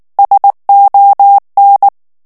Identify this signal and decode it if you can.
Morse code, heard as a keyed tone, about 19 wpm, 800 Hz. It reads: SON
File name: FRANCAIS TEXTE CLAIR 2 VITESSE 18 MOTS MINUTES